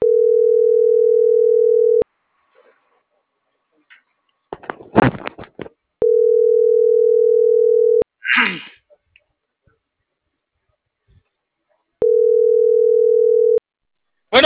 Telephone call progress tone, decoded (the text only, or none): ringback tone